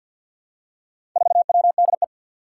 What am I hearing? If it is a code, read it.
Morse code, heard as a keyed tone, about 50 wpm, 700 Hz. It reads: 4WDE
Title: Four Challenging Letter-Number Combinations 50wpm